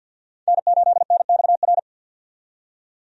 Morse code, heard as a keyed tone, about 52 wpm, 700 Hz. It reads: N8NXR
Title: USA Call Signs - 5 Characters - 50wpm